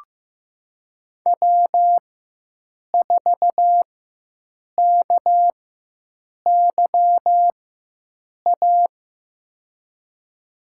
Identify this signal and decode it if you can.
Morse code, heard as a keyed tone, about 16 wpm, spaced as for 7 wpm, 700 Hz. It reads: W4KYA
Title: USA Call Signs - 5 Characters - 4x Character Spacing - 15wpm